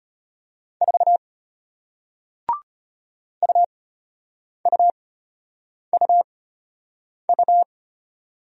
Morse code, heard as a keyed tone, about 33 wpm, spaced as for 8 wpm, 700 Hz. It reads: 4 UUUU